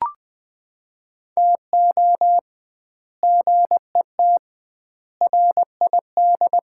Morse code, heard as a keyed tone, about 21 wpm, 700 Hz. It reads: TO GET RID
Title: 2x Word-Spacing - Sets of 5 Words - 20wpm